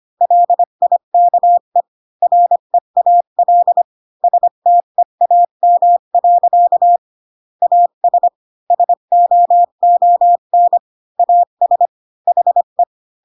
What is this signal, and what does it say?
Morse code, heard as a keyed tone, about 25 wpm, 700 Hz. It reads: LIKE REAL STEAM. AS SOON AS HE
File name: Pinocchio / Pinocchio ch3 20 25 ik1hge